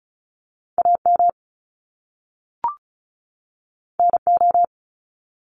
Morse code, heard as a keyed tone, about 35 wpm, 700 Hz. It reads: AM NO
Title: Top Two Letter Combinations in Words 35wpm